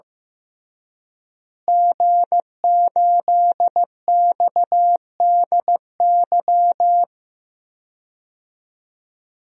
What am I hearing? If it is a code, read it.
Morse code, heard as a keyed tone, about 15 wpm, 700 Hz. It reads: G8XDY
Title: DXCC-100 Call Signs - 5 Characters - Repeated 3x - 15wpm